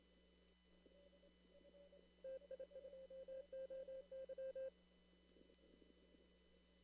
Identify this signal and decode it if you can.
Morse code, heard as a keyed tone, about 27 wpm, 530 Hz. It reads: TI2OY
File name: CW / kt5j wpx cw-op-k5pi